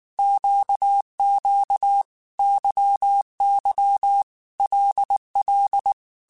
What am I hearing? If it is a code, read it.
Morse code, heard as a keyed tone, about 19 wpm, 800 Hz. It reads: QQ YY LL
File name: Groups@18 / Groups@18 mp3 / LFCYQ@18